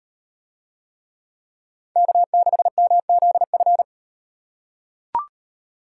Morse code, heard as a keyed tone, about 38 wpm, 700 Hz. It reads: K6MZF